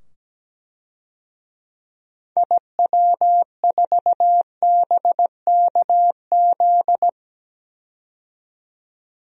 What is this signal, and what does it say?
Morse code, heard as a keyed tone, about 17 wpm, 700 Hz. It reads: IW4BKZ
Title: DXCC-100 Call Signs - 6 Characters - 17wpm